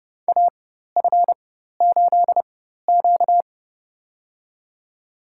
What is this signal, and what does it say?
Morse code, heard as a keyed tone, about 30 wpm, spaced as for 14 wpm, 700 Hz. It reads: AF8Q
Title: USA Call Signs - 4 Characters - 4x Character Spacing - 30wpm